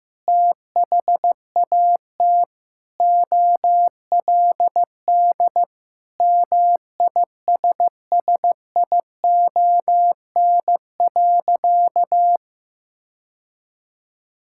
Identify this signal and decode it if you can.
Morse code, heard as a keyed tone, about 15 wpm, 700 Hz. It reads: THAT OLD MISSION.
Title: Sentences from Top 1600 Words 15wpm